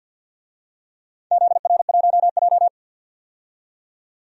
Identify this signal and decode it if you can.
Morse code, heard as a keyed tone, about 52 wpm, 700 Hz. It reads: ZR1J
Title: DXCC-100 Call Signs - 4 Characters - 50wpm